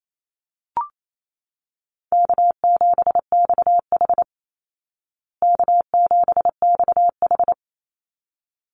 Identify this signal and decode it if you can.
Morse code, heard as a keyed tone, about 28 wpm, 700 Hz. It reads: K7XH K7XH
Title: Call Signs Repeated 3x - 28wpm